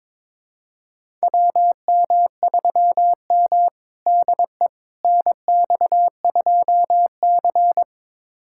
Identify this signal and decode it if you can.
Morse code, heard as a keyed tone, about 22 wpm, 700 Hz. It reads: WM3M DE NX2C
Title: USA Callsign DE Callsign - 4 Characters - 22wpm